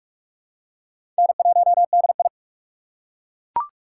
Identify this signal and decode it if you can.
Morse code, heard as a keyed tone, about 41 wpm, 700 Hz. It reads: N1DI
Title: USA Call Signs - 4 Characters - 6x Character Spacing - 45wpm